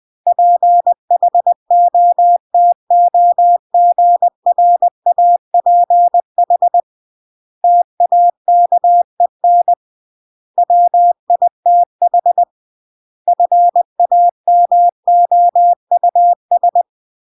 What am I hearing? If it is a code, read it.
Morse code, heard as a keyed tone, about 20 wpm, 700 Hz. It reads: PHOTOGRAPH TAKEN WITH FAMOUS
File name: CW / jodorowsky / 82 commandments 20-20